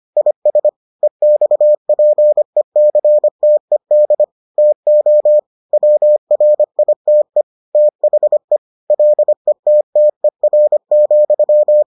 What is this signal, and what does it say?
Morse code, heard as a keyed tone, about 25 wpm, 600 Hz. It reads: IS EXPECTED TO WRITE THE LETTER,